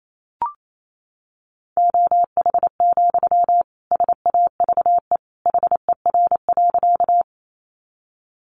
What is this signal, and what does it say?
Morse code, heard as a keyed tone, about 28 wpm, 700 Hz. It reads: OH, SAVE HER.
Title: Sentences from Top 1100 Words 28wpm